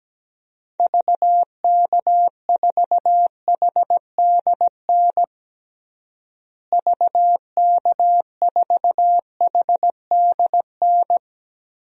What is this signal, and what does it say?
Morse code, heard as a keyed tone, about 17 wpm, 700 Hz. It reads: VK4HDN VK4HDN
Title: DXCC-100 Call Signs - 6 Characters - Repeated 3x - 17wpm